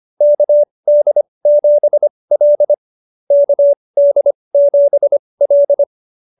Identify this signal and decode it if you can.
Morse code, heard as a keyed tone, about 25 wpm, spaced as for 20 wpm, 590 Hz. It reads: KD7L KD7L